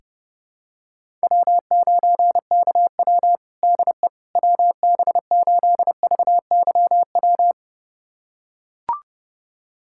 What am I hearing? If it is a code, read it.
Morse code, heard as a keyed tone, about 30 wpm, 700 Hz. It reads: W9KW DE WB8VYW
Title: USA Callsign DE Callsign 30wpm